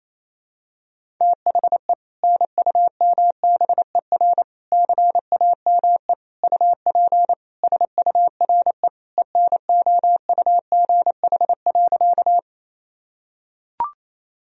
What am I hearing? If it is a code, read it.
Morse code, heard as a keyed tone, about 28 wpm, 700 Hz. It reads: THE NUMBER CAME UP SURE ENOUGH.
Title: Sentences from Top 1000 Words - Review Part 1 - 28wpm